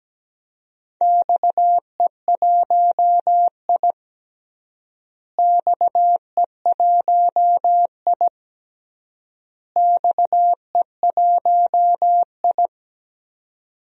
Morse code, heard as a keyed tone, about 17 wpm, 700 Hz. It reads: XE1I XE1I XE1I
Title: DXCC-100 Call Signs - 4 Characters - Repeated 3x - 17wpm